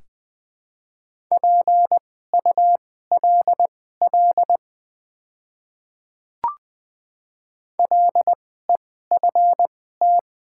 Morse code, heard as a keyed tone, about 20 wpm, spaced as for 15 wpm, 700 Hz. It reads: PULL LEFT
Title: 2x Character Spacing - Top 500 Words - 20wpm